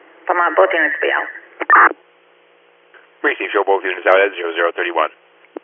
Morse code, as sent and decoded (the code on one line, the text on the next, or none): none